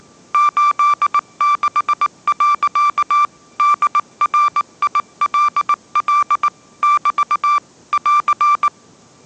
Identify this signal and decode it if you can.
Morse code, heard as a keyed tone, about 19 wpm, 1180 Hz. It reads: 86. DRILL <BT> <AR>